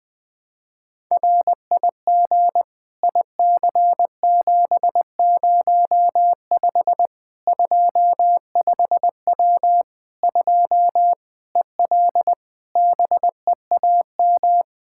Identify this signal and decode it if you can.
Morse code, heard as a keyed tone, about 20 wpm, 700 Hz. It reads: RIG IC705 25W 2 EL BEAM
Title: Practice QSO - Exchange 2 - Info - 20wpm